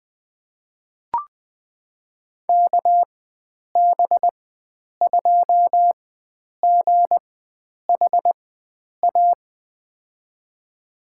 Morse code, heard as a keyed tone, about 20 wpm, spaced as for 9 wpm, 700 Hz. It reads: KB2GHA
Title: USA Call Signs - 4x Character Spacing - 20wpm